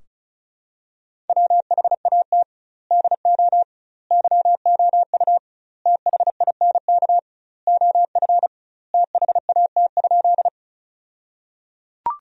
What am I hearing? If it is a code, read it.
Morse code, heard as a keyed tone, about 36 wpm, 700 Hz. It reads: WHAT DO YOU THINK OF THAT?
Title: 2x Word-Spacing - Sentences from Top 200 Words - 35wpm